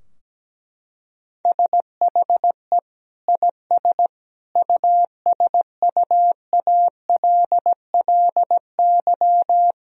Morse code, heard as a keyed tone, about 17 wpm, 700 Hz. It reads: SHE IS USUALLY
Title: Sentences from Top 1000 Words - Review Part 3 - 17wpm